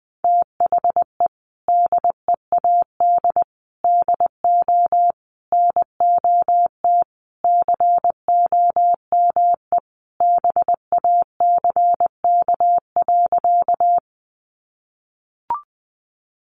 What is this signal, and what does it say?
Morse code, heard as a keyed tone, about 20 wpm, 700 Hz. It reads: THE DEAD DO NOT COME BACK.
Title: Sentences from Top 1500 Words 20wpm